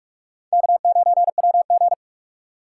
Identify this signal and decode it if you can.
Morse code, heard as a keyed tone, about 47 wpm, 690 Hz. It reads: K9WG
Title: USA Call Signs - 4 Characters - 8x Character Spacing - 45wpm